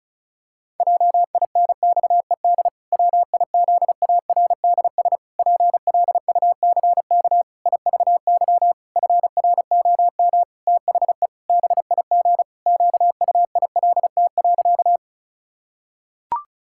Morse code, heard as a keyed tone, about 36 wpm, 700 Hz. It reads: JINXED WIZARDS PLUCK IVY FROM THE BIG QUILT.
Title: Pangram Challenge v2 - 35wpm